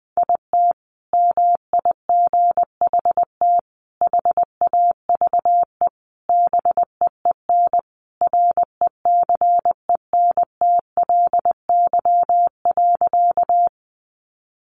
Morse code, heard as a keyed tone, about 20 wpm, 700 Hz. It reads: IT MIGHT HAVE BEEN RECENTLY.
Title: Sentences from Top 1200 Words 20wpm